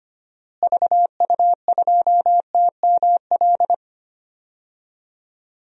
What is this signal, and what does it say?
Morse code, heard as a keyed tone, about 25 wpm, 700 Hz. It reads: VU2TML
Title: DXCC-100 Call Signs - 6 Characters - Repeated 2x - 25wpm